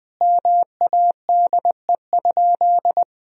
Morse code, heard as a keyed tone, about 20 wpm, 700 Hz. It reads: MADE?
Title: Sentences from Top 1000 Words - Review Part 4 - 20wpm